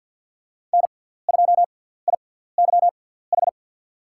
Morse code, heard as a keyed tone, about 52 wpm, spaced as for 17 wpm, 700 Hz. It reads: N2IYH
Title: USA Call Signs - 5 Characters - 6x Character Spacing - 50wpm